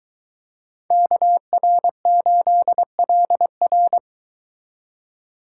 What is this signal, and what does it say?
Morse code, heard as a keyed tone, about 22 wpm, 700 Hz. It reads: KR8LR